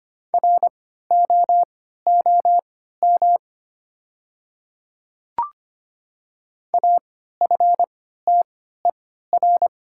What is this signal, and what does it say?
Morse code, heard as a keyed tone, about 25 wpm, spaced as for 14 wpm, 700 Hz. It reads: ROOM AFTER
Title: 3x Character Spacing - Top 500 Words - 25wpm